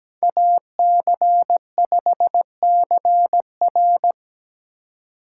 Morse code, heard as a keyed tone, about 17 wpm, 700 Hz. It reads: AC5CR
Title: USA Call Signs - 5 Characters - 4x Character Spacing - 17wpm